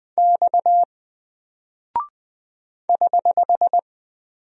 Morse code, heard as a keyed tone, about 20 wpm, 700 Hz. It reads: X <HH>
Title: Single Character (Letter:Number:Punctuation:Prosign) Alt 20wpm